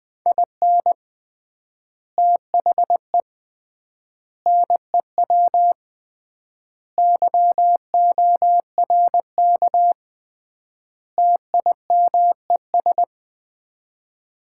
Morse code, heard as a keyed tone, about 20 wpm, 700 Hz. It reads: IN THE NEW YORK TIMES
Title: 3x Word-Spacing - Sets of 5 Words - 20wpm